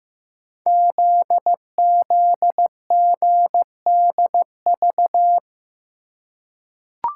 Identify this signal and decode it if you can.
Morse code, heard as a keyed tone, about 15 wpm, 700 Hz. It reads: ZZGDV
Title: Code Groups - 5 Letters - 15wpm